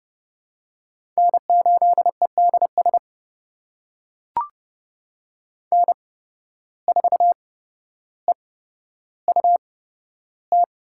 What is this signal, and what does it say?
Morse code, heard as a keyed tone, about 30 wpm, spaced as for 13 wpm, 700 Hz. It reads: N8EDS N4EUT
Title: USA Call Signs - 5 Characters - 8x Character Spacing - 30wpm